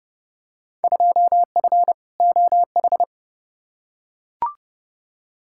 Morse code, heard as a keyed tone, about 30 wpm, 700 Hz. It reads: 2F OH
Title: Field Day Exchanges - 30wpm